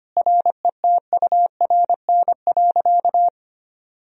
Morse code, heard as a keyed tone, about 25 wpm, 700 Hz. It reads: RETURN.
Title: Sentences from Top 1000 Words - Review Part 1 - 25wpm